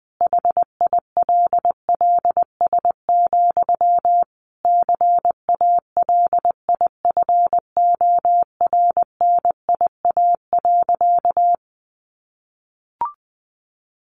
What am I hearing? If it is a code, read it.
Morse code, heard as a keyed tone, about 20 wpm, 700 Hz. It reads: HILLS, CALIFORNIA.
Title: Easy Phrases 20wpm